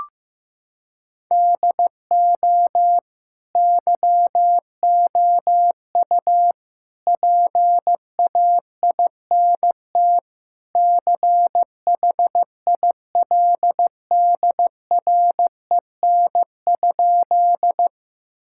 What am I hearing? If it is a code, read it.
Morse code, heard as a keyed tone, about 15 wpm, 700 Hz. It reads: DO YOU PAINT CHILDREN?
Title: Sentences from Top 1000 Words - Review Part 6 - 15wpm